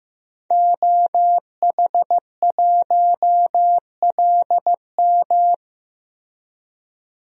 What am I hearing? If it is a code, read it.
Morse code, heard as a keyed tone, about 15 wpm, 700 Hz. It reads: OH1LM
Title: DXCC-100 Call Signs - 5 Characters - Repeated 2x - 15wpm